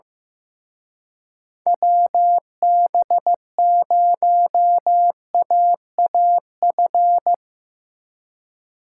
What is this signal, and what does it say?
Morse code, heard as a keyed tone, about 17 wpm, 700 Hz. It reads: WB0AAF
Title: Speed Racing - USA 2x3 Call Signs 15wpm